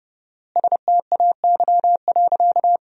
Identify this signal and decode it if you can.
Morse code, heard as a keyed tone, about 31 wpm, 700 Hz. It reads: STAY.
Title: 4x Word-Spacing - Sentences from Top 500 Words - 30wpm